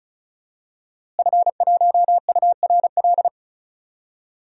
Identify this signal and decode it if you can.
Morse code, heard as a keyed tone, about 36 wpm, 700 Hz. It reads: F1URL
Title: DXCC-100 Call Signs - 5 Characters - Repeated 3x - 35wpm